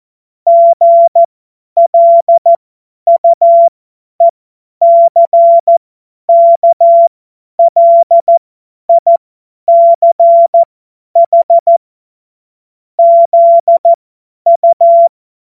Morse code, heard as a keyed tone, about 14 wpm, spaced as for 10 wpm, 680 Hz. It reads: GLUECKLICH ZU
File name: Morsekurs / Witze / Witz 05. Oktober 2021, 50 F70